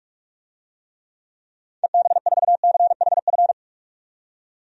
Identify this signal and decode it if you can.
Morse code, heard as a keyed tone, about 46 wpm, 700 Hz. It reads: EB4CHF
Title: DXCC-100 Call Signs - 6 Characters - 45wpm